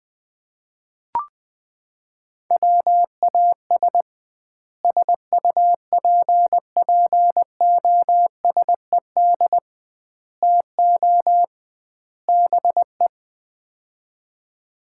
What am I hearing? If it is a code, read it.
Morse code, heard as a keyed tone, about 20 wpm, 700 Hz. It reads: WAS SUPPOSED TO BE
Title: 2x Word-Spacing - Sets of 4 Words - 20wpm